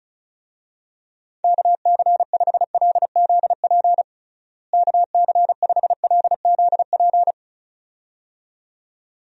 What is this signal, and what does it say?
Morse code, heard as a keyed tone, about 34 wpm, 700 Hz. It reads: KC5LZP KC5LZP
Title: USA Call Signs - 6 Characters - Repeated 2x - 35wpm